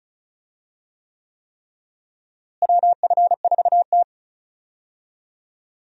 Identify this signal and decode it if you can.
Morse code, heard as a keyed tone, about 34 wpm, 700 Hz. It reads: WF4T
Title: USA Call Signs - 4 Characters - Repeated 3x - 35wpm